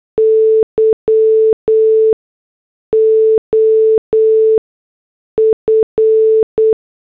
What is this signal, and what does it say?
Morse code, heard as a keyed tone, about 8 wpm, 440 Hz. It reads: Y O F
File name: morse / headlines